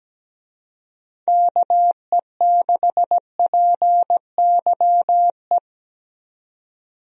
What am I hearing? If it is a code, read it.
Morse code, heard as a keyed tone, about 17 wpm, 700 Hz. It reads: KE6PYE
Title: USA Call Signs - 6 Characters - 17wpm